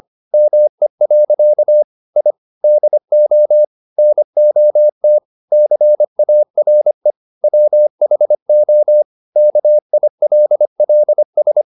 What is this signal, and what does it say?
Morse code, heard as a keyed tone, about 25 wpm, 600 Hz. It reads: ME. I DO NOT CARE WHO KILLS